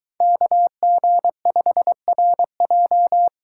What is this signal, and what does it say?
Morse code, heard as a keyed tone, about 21 wpm, 700 Hz. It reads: KG5RJ